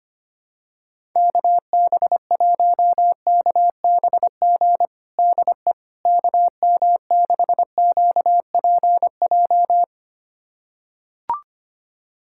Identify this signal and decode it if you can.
Morse code, heard as a keyed tone, about 25 wpm, 700 Hz. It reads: KB1KBG DE KM6QPJ
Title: USA Callsign DE Callsign 25wpm